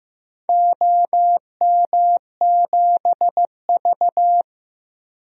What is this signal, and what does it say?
Morse code, heard as a keyed tone, about 15 wpm, 700 Hz. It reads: OM7V
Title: DXCC-100 Call Signs - 4 Characters - 15wpm